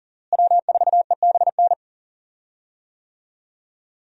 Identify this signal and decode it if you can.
Morse code, heard as a keyed tone, about 40 wpm, 700 Hz. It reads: W4EBN
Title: USA Call Signs - 5 Characters - Repeated 2x - 40wpm